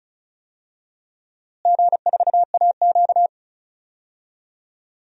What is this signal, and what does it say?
Morse code, heard as a keyed tone, about 36 wpm, 700 Hz. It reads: G4AQ
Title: DXCC-100 Call Signs - 4 Characters - Repeated 3x - 35wpm